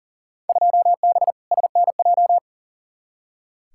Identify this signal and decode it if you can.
Morse code, heard as a keyed tone, about 40 wpm, 700 Hz. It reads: 2B SNJ
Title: Field Day Exchanges Encore - 40wpm